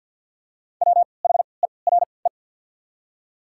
Morse code, heard as a keyed tone, about 54 wpm, 700 Hz. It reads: W H E R E
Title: 3x Character Spacing - Top 300 Words - 50wpm